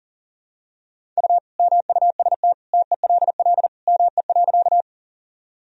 Morse code, heard as a keyed tone, about 40 wpm, 700 Hz. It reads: U MUST TELL ME.
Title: Sentences from Top 200 Words and Abbreviations 40wpm